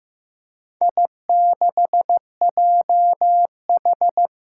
Morse code, heard as a keyed tone, about 15 wpm, 700 Hz. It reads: I6JH
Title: Four Challenging Letter-Number Combinations 15wpm